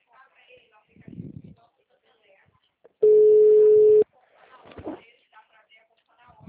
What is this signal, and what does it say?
Ringback tone (Europe, 425 Hz), with speech.
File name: out- -2015- - - .41349